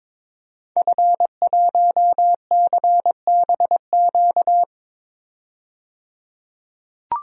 Morse code, heard as a keyed tone, about 21 wpm, 700 Hz. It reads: F1CBQ
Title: DXCC-100 Call Signs - 5 Characters - 22wpm